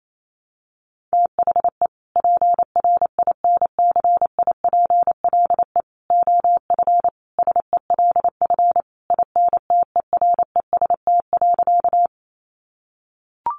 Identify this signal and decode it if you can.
Morse code, heard as a keyed tone, about 28 wpm, 700 Hz. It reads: THE PRINCIPLE OF SELF INTEREST.
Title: Sentences from Top 1400 Words 28wpm